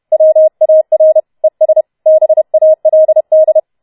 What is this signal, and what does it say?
Morse code, heard as a keyed tone, about 31 wpm, 630 Hz. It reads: WAR ES BALD